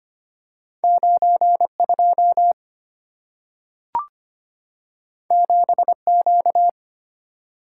Morse code, heard as a keyed tone, about 25 wpm, 700 Hz. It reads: 92 7Q
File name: Two Letter-Number Combinations - 25wpm